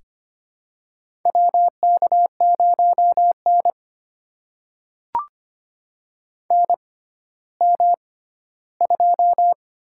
Morse code, heard as a keyed tone, about 25 wpm, spaced as for 14 wpm, 700 Hz. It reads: WK0N NM2
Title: USA Call Signs - 4 Characters - 6x Character Spacing - 25wpm